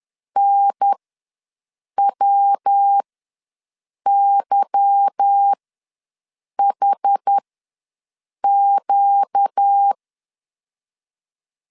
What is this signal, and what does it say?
Morse code, heard as a keyed tone, about 10 wpm, spaced as for 6 wpm, 800 Hz. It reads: NWYHQ
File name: LEZIONI / 8 YJ45 / 8 Y ws 10wpm 3952